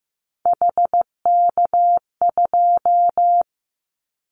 Morse code, heard as a keyed tone, about 15 wpm, 700 Hz. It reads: HK2
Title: Callsign Prefixes for Top 100 DXCC - 15wpm